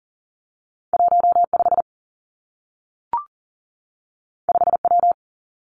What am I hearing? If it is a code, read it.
Morse code, heard as a keyed tone, about 38 wpm, 700 Hz. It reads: 1 5 5 W
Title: Two Challenging Letter-Number Combinations 40wpm